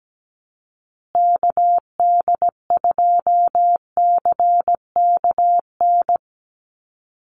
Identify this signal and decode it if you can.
Morse code, heard as a keyed tone, about 17 wpm, 700 Hz. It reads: KD2CKN
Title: USA Call Signs Repeated 2x 17wpm